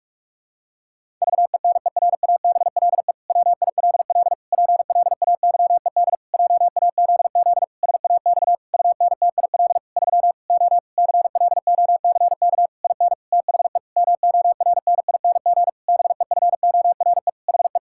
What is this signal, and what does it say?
Morse code, heard as a keyed tone, about 46 wpm, 700 Hz. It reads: VENERABLE WILL PLAYED JAZZ SAX UNTIL 3 O CLOCK IN THE MORNING BEFORE HE
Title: Pangram Challenge v2 - 45wpm